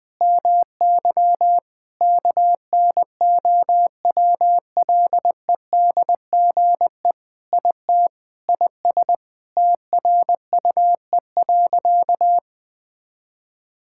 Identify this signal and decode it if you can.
Morse code, heard as a keyed tone, about 20 wpm, 700 Hz. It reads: MY KNOWLEDGE IT IS TRUE.
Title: Sentences with 6-Word Phrases - Spoken Beginning - 20wpm